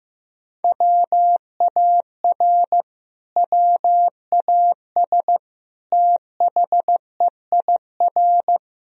Morse code, heard as a keyed tone, about 16 wpm, 700 Hz. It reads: WAR WAS THEIR
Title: Sentences from Top 1000 Words - Review Part 3 - 15wpm